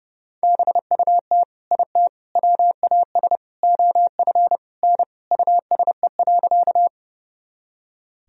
Morse code, heard as a keyed tone, about 30 wpm, 700 Hz. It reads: BUT IT WAS OF N USE.
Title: Sentences from Top 100 Words and Abbreviations 30wpm